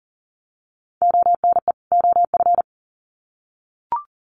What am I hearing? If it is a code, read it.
Morse code, heard as a keyed tone, about 40 wpm, 700 Hz. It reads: ONE OF
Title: Sets of 2 Words 40wpm